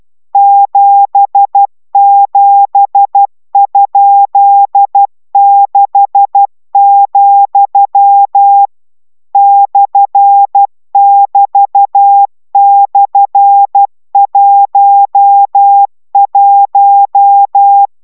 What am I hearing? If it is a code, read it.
Morse code, heard as a keyed tone, about 12 wpm, 800 Hz. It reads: 77?6, /<BT>/11